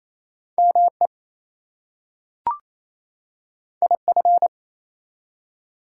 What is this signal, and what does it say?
Morse code, heard as a keyed tone, about 27 wpm, 700 Hz. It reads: ME IF
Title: Two Letter Words - 28wpm